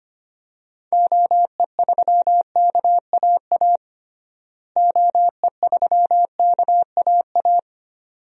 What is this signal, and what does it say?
Morse code, heard as a keyed tone, about 25 wpm, 700 Hz. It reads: OE3KAA OE3KAA
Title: DXCC-100 Call Signs - 6 Characters - Repeated 2x - 25wpm